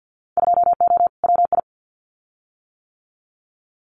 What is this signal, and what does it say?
Morse code, heard as a keyed tone, about 50 wpm, 700 Hz. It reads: 2O WI
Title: Winter Field Day Exchanges 50wpm